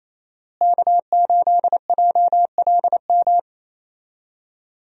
Morse code, heard as a keyed tone, about 28 wpm, 700 Hz. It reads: K8JLM
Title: USA Call Signs - 5 Characters - Repeated 2x - 28wpm